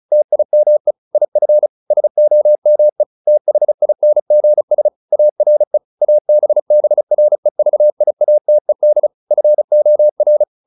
Morse code, heard as a keyed tone, about 34 wpm, 600 Hz. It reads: TIME IF SOME THINGS ARE ABBREVIATED FOR